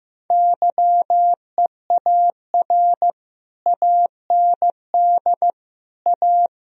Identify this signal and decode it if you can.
Morse code, heard as a keyed tone, about 15 wpm, 700 Hz. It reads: YEAR AND A
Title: Speed-Racing - Sets of 4 Words 15wpm